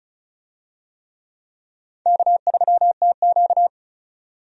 Morse code, heard as a keyed tone, about 34 wpm, 700 Hz. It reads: K3TQ
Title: USA Call Signs - 4 Characters - 35wpm